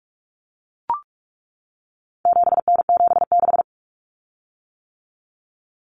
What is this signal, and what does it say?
Morse code, heard as a keyed tone, about 44 wpm, 700 Hz. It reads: 7N76
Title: Common Cut Numbers - 4-Digit 45wpm